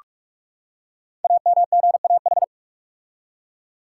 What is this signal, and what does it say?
Morse code, heard as a keyed tone, about 44 wpm, 700 Hz. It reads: AMGAH
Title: Code Groups - 5 Letters - 45wpm